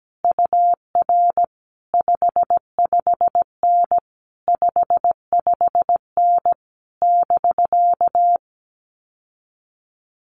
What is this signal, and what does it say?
Morse code, heard as a keyed tone, about 17 wpm, 700 Hz. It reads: UR 55N 55N <BK>
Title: Common QSO Phrases Encore 17wpm